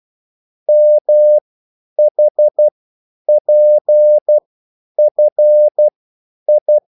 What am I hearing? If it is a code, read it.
Morse code, heard as a keyed tone, about 12 wpm, spaced as for 9 wpm, 600 Hz. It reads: MHPFI